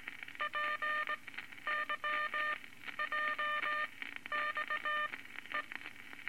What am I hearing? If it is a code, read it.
Morse code, heard as a keyed tone, about 19 wpm, spaced as for 12 wpm, 1200 Hz, with noise. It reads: PYJXE
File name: Morse Code Records / 19 A LCM 166 Ten Words Per Minute